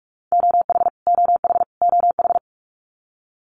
Morse code, heard as a keyed tone, about 44 wpm, 700 Hz. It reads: OH OH OH
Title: US States Repeated 3x 45wpm